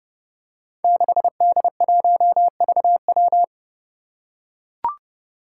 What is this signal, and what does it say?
Morse code, heard as a keyed tone, about 30 wpm, 700 Hz. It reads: 6D1VW
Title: Five Challenging Letter-Number Combinations 30wpm